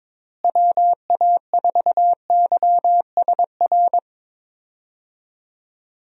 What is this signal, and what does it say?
Morse code, heard as a keyed tone, about 22 wpm, 700 Hz. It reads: WA4YSR
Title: USA Call Signs - 6 Characters - Repeated 3x - 22wpm